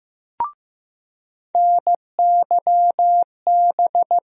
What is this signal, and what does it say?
Morse code, heard as a keyed tone, about 15 wpm, 700 Hz. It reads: NYB
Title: USA Call Sign Postfixes - 3 Letters 15wpm